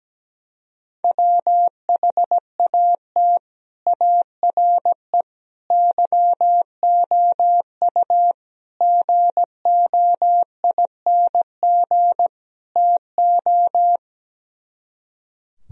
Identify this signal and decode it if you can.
Morse code, heard as a keyed tone, about 17 wpm, 700 Hz. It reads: WHAT ARE YOU GOING TO
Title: Speed-Racing - Sets of 5 Words 17wpm